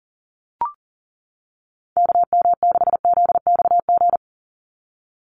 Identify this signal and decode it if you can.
Morse code, heard as a keyed tone, about 40 wpm, 700 Hz. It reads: KM6ZXG
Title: USA Call Signs 40wpm